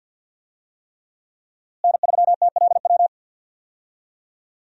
Morse code, heard as a keyed tone, about 51 wpm, 700 Hz. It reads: N3TLW
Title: USA Call Signs - 5 Characters - Repeated 3x - 50wpm